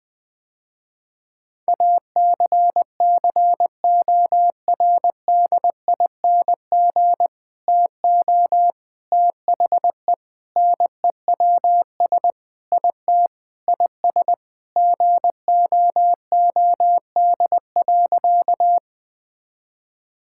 Morse code, heard as a keyed tone, about 20 wpm, 700 Hz. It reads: ACCORDING TO THE NEWS IT IS GOOD.
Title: Sentences with 3-Word Phrases - Spoken Beginning - 20wpm